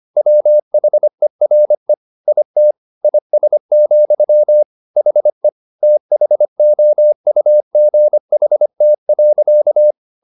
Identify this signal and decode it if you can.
Morse code, heard as a keyed tone, about 25 wpm, 600 Hz. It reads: WHERE IT IS, HE THOUGHT.